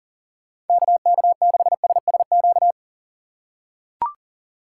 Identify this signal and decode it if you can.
Morse code, heard as a keyed tone, about 40 wpm, 700 Hz. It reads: KK6SSQ
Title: USA Call Signs - 6 Characters - 6x Character Spacing - 40wpm